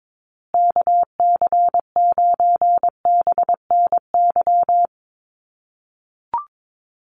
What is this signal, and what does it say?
Morse code, heard as a keyed tone, about 22 wpm, 700 Hz. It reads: KC9BNY
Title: USA Call Signs Repeated 2x 22wpm